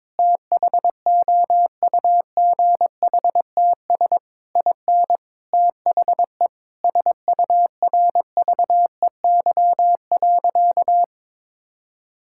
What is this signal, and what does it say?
Morse code, heard as a keyed tone, about 22 wpm, 700 Hz. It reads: THOUGHTS IN THE SURVEY.
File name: Sentences with 7-Word Phrases - 22wpm